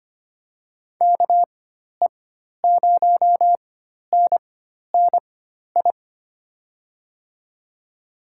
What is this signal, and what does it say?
Morse code, heard as a keyed tone, about 25 wpm, spaced as for 12 wpm, 700 Hz. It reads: KE0NNI
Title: USA Call Signs - 4x Character Spacing - 25wpm